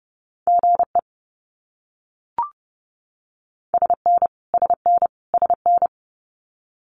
Morse code, heard as a keyed tone, about 30 wpm, 700 Hz. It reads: GE SN SN SN
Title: Abbreviations Repeated 3x - 30wpm